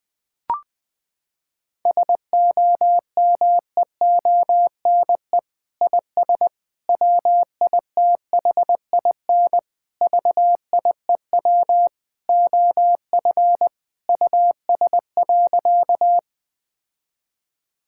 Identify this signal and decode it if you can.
Morse code, heard as a keyed tone, about 20 wpm, 700 Hz. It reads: SOMEONE IS WITHIN VIEW OF US.
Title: Sentences from Top 1000 Words - Review Part 6 - 20wpm